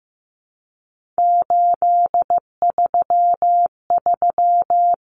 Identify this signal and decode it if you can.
Morse code, heard as a keyed tone, about 15 wpm, 700 Hz. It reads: 833
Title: Common Cut Numbers - 3-Digit 15wpm